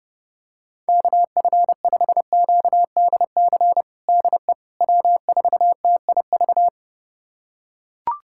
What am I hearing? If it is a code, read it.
Morse code, heard as a keyed tone, about 30 wpm, 700 Hz. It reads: KF5QDC DE W4TIV
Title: USA Callsign DE Callsign 30wpm